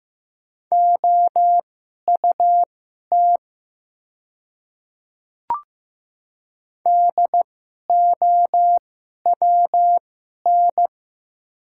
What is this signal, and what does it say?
Morse code, heard as a keyed tone, about 15 wpm, spaced as for 11 wpm, 700 Hz. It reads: OUT DOWN
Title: 2x Character Spacing - Top 100 Words - 15wpm